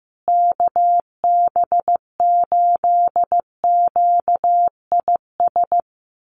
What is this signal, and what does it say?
Morse code, heard as a keyed tone, about 15 wpm, 700 Hz. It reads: KB8QIS
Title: USA Call Signs Repeated 3x 15wpm